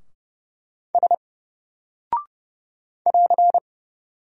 Morse code, heard as a keyed tone, about 29 wpm, 700 Hz. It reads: S <AR>
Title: Single Character (Letter:Number:Punctuation:Prosign) Alt 30wpm